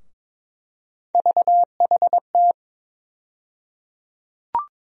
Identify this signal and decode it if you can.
Morse code, heard as a keyed tone, about 22 wpm, 700 Hz. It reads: VHT
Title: USA Call Sign Postfixes - 3 Letters 22wpm